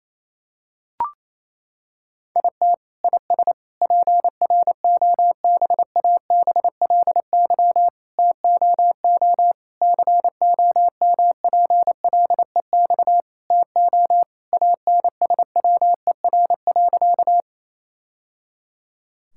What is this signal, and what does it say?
Morse code, heard as a keyed tone, about 28 wpm, 700 Hz. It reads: IT IS PROBABLY TOO COMPLEX TO ANSWER.
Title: Sentences from Top 1000 Words - Review Part 6 - 28wpm